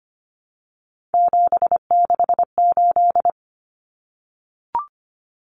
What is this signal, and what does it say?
Morse code, heard as a keyed tone, about 25 wpm, 700 Hz. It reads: 768
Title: Common Cut Numbers - 3-Digit 25wpm